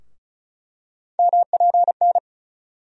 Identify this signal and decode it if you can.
Morse code, heard as a keyed tone, about 36 wpm, 700 Hz. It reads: MPN
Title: USA Call Sign Postfixes - 3 Letters 35wpm